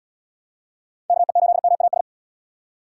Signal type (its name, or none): Morse code